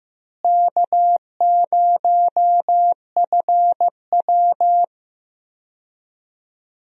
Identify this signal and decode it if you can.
Morse code, heard as a keyed tone, about 15 wpm, 700 Hz. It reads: K0FW
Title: USA Call Signs - 4 Characters - Repeated 3x - 15wpm